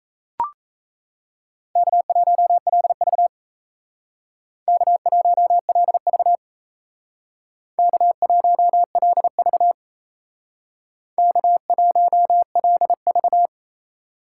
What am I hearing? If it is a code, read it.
Morse code, heard as a keyed tone, about 37 wpm, 700 Hz. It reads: K1LV K1LV K1LV K1LV